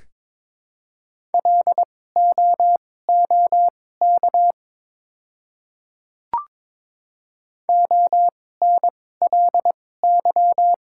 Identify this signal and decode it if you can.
Morse code, heard as a keyed tone, about 22 wpm, spaced as for 16 wpm, 700 Hz. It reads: LOOK ONLY
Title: 2x Character Spacing - Top 200 Words - 22wpm